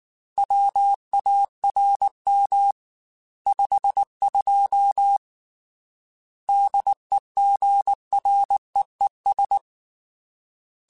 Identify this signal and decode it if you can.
Morse code, heard as a keyed tone, about 19 wpm, 800 Hz. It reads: WARM 52 DEGREES
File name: Weather Sets / Weather Sets-mp3s / Weather 4 10 @18